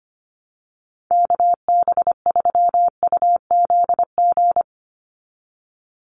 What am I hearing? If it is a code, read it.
Morse code, heard as a keyed tone, about 25 wpm, 700 Hz. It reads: KB3UZG
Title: USA Call Signs 25wpm